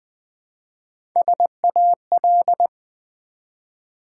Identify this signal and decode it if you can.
Morse code, heard as a keyed tone, about 20 wpm, 700 Hz. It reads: SAL
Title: Build A Word - 1 Letter At A Time - 4-Letter Words - 20wpm